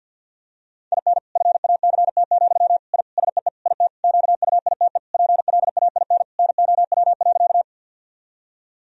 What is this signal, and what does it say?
Morse code, heard as a keyed tone, about 50 wpm, 700 Hz. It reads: IN FACT, I SEE IT QUITE PLAIN NOW.
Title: Sentences from Top 1000 Words - Review Part 3 - 50wpm